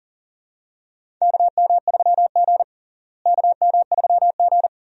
Morse code, heard as a keyed tone, about 37 wpm, 700 Hz. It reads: KM3G KM3G
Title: USA Call Signs - 4 Characters - Repeated 2x - 40wpm